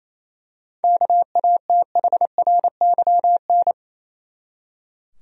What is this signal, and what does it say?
Morse code, heard as a keyed tone, about 28 wpm, 700 Hz. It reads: KATHRYN
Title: USA Common Names - 28wpm